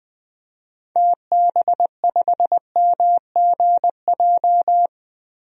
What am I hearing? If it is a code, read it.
Morse code, heard as a keyed tone, about 21 wpm, 700 Hz. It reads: TB5MGJ
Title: DXCC-100 Call Signs - 6 Characters - Repeated 3x - 20wpm